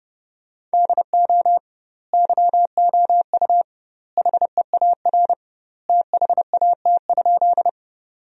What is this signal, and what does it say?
Morse code, heard as a keyed tone, about 30 wpm, 700 Hz. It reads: DO YOU HEAR THAT?
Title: 2x Word-Spacing - Sentences from Top 300 Words - 30wpm